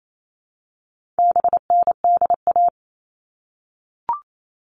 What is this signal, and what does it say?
Morse code, heard as a keyed tone, about 28 wpm, 700 Hz. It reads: BNDA
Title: All Cut Numbers - 4-Digit 28wpm